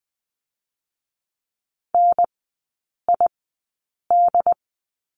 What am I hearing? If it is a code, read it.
Morse code, heard as a keyed tone, about 20 wpm, spaced as for 8 wpm, 700 Hz. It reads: NID
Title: 087 Lesson 15e - Number 5 - Three Characters 20wpm